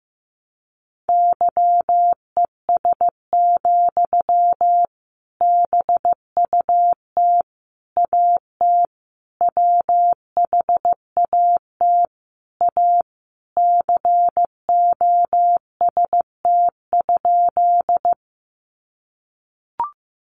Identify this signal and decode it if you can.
Morse code, heard as a keyed tone, about 15 wpm, 700 Hz. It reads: YES, BUT AT WHAT A COST?
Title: Sentences from Top 800 Words 15wpm